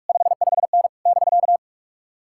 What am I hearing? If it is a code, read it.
Morse code, heard as a keyed tone, about 56 wpm, 700 Hz. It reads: 55N <BK>
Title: Speed-Racing Common QSO Phrases - 45wpm